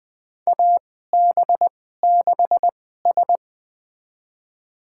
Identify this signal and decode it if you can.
Morse code, heard as a keyed tone, about 20 wpm, spaced as for 14 wpm, 700 Hz. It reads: AB6S
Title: USA Call Signs - 2x Character Spacing - 20wpm